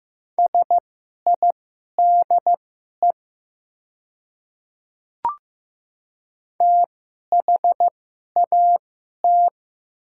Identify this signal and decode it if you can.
Morse code, heard as a keyed tone, about 15 wpm, spaced as for 11 wpm, 700 Hz. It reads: SIDE THAT
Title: 2x Character Spacing - Top 200 Words - 15wpm